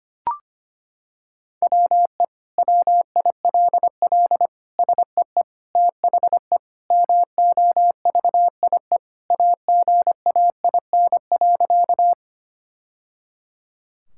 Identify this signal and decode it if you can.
Morse code, heard as a keyed tone, about 25 wpm, 700 Hz. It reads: WE WILL SEE THE MOVIE AGAIN.
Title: Sentences from Top 1000 Words - Review Part 5 - 25wpm